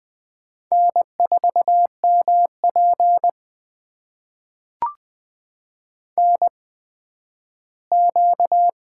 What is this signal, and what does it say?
Morse code, heard as a keyed tone, about 20 wpm, spaced as for 12 wpm, 700 Hz. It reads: N4MP NQ
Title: USA Call Signs - 4 Characters - 8x Character Spacing - 20wpm